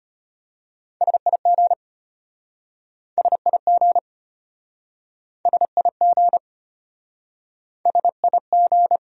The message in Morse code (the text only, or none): SIG SIG SIG SIG